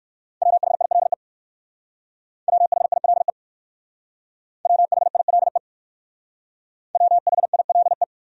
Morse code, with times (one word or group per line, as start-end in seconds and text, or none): 0.4-1.1 WHILE
2.5-3.3 WHILE
4.7-5.6 WHILE
7.0-8.0 WHILE